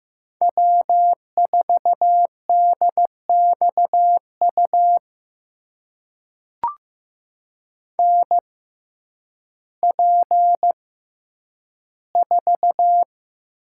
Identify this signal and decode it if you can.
Morse code, heard as a keyed tone, about 15 wpm, spaced as for 9 wpm, 700 Hz. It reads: W4DXU NP4
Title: USA Call Signs - 5 Characters - 6x Character Spacing - 15wpm